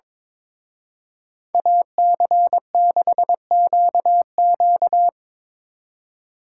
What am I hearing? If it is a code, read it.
Morse code, heard as a keyed tone, about 25 wpm, 700 Hz. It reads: AC6QQ
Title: Speed Racing - USA Call Signs 22wpm